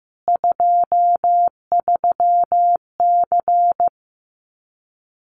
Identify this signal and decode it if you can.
Morse code, heard as a keyed tone, about 15 wpm, 700 Hz. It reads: 23C
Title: Common QSO Phrases 15wpm